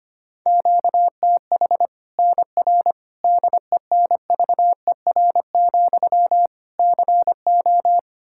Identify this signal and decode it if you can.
Morse code, heard as a keyed tone, about 25 wpm, 700 Hz. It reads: QTH NR DENVER, CO
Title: K1USN SST QSO Practice 25wpm